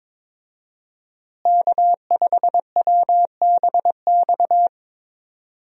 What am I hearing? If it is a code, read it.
Morse code, heard as a keyed tone, about 22 wpm, 700 Hz. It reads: K5WBX
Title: USA Call Signs - 5 Characters - Repeated 2x - 22wpm